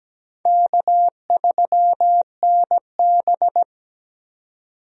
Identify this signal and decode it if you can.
Morse code, heard as a keyed tone, about 17 wpm, 700 Hz. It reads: K3NB
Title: USA Call Signs - 4 Characters - 17wpm